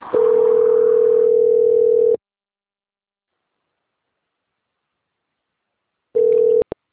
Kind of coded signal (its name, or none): telephone call progress tone